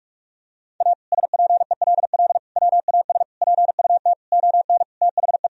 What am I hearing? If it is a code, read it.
Morse code, heard as a keyed tone, about 46 wpm, 700 Hz. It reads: A SPELL WAS PUT ON THE
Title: Sentences from Top 1000 Words - Review Part 1 - 45wpm